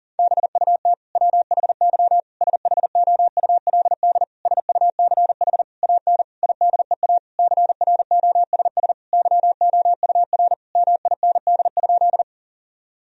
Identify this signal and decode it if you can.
Morse code, heard as a keyed tone, about 37 wpm, 700 Hz. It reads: BUT WHY SHOULD SUCH AN IDEA CROSS YOUR MIND?
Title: Sentences from Top 1000 Words - Review Part 1 - 40wpm